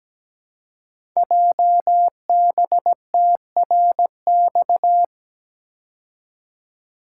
Morse code, heard as a keyed tone, about 17 wpm, 700 Hz. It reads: JBTRX
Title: Code Groups - 5 Letters - 17wpm